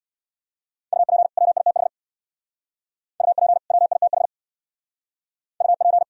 Morse code, heard as a keyed tone, about 66 wpm, 700 Hz. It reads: UP WITH UP WITH UP